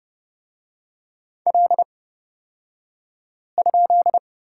Morse code, heard as a keyed tone, about 30 wpm, spaced as for 5 wpm, 700 Hz. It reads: L?
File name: Lesson 37d - Letter x - Two Characters 30wpm